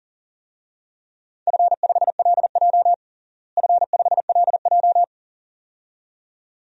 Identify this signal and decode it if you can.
Morse code, heard as a keyed tone, about 40 wpm, 700 Hz. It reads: F5LJ F5LJ
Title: DXCC-100 Call Signs - 4 Characters - Repeated 2x - 40wpm